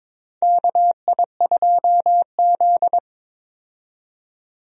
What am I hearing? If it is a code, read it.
Morse code, heard as a keyed tone, about 22 wpm, 700 Hz. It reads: KI2Z
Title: USA Call Signs - 4 Characters - 22wpm